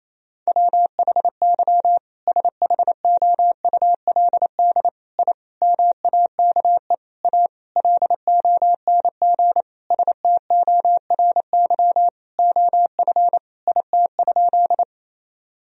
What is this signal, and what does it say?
Morse code, heard as a keyed tone, about 28 wpm, 700 Hz. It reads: WHY SHOULD I MAKE A LONG STORY OF IT?
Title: Sentences from Top 1000 Words - Review Part 1 - 28wpm